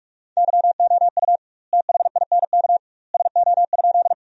Morse code, heard as a keyed tone, about 46 wpm, 700 Hz. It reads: YOU THINK SO?
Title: 2x Word-Spacing - Sentences from Top 200 Words - 45wpm